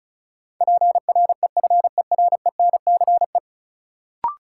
Morse code, heard as a keyed tone, about 36 wpm, 700 Hz. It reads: PREFERENCE
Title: Sentences from Top 1500 Words - New Word Drill - 35wpm